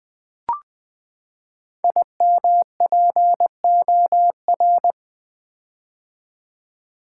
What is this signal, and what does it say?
Morse code, heard as a keyed tone, about 20 wpm, 700 Hz. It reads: IMPOR
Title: Build A Word - 5 Letters At A Time - 10-Letter Words - 20wpm